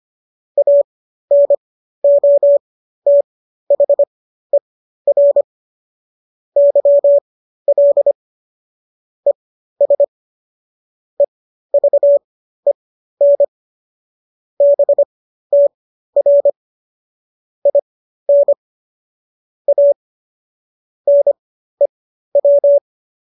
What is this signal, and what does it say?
Morse code, heard as a keyed tone, about 25 wpm, spaced as for 13 wpm, 570 Hz. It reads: ANOTHER YL ES EVEN BTR IN A NEW